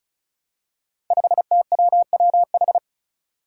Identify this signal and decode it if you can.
Morse code, heard as a keyed tone, about 36 wpm, 700 Hz. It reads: 5TWWH
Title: Five Challenging Letter-Number Combinations 35wpm